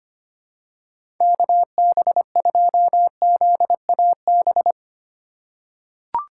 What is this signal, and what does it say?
Morse code, heard as a keyed tone, about 25 wpm, 700 Hz. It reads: KB2ZAB
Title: USA Call Signs - 6x Character Spacing - 25wpm